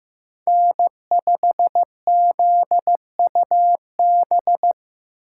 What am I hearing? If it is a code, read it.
Morse code, heard as a keyed tone, about 15 wpm, 700 Hz. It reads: N5ZUB
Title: USA Call Signs - 5 Characters - 3x Character Spacing - 15wpm